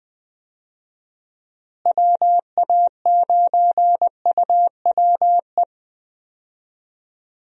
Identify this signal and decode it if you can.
Morse code, heard as a keyed tone, about 20 wpm, 700 Hz. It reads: WA9UWE
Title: USA Call Signs - 6 Characters - Repeated 3x - 20wpm